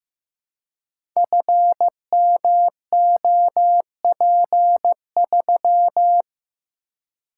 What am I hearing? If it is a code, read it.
Morse code, heard as a keyed tone, about 15 wpm, 700 Hz. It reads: FMOP3
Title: Code Groups - 5 Letter:Numbers - 15wpm